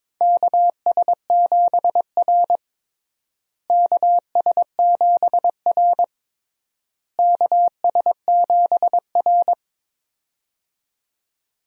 Morse code, heard as a keyed tone, about 22 wpm, 700 Hz. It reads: KS7R KS7R KS7R
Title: USA Call Signs - 4 Characters - Repeated 3x - 22wpm